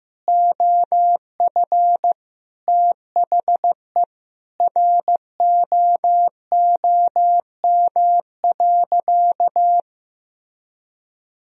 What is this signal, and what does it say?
Morse code, heard as a keyed tone, about 16 wpm, 700 Hz. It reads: OF THE ROOM.
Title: Sentences from Top 1000 Words - Review Part 3 - 15wpm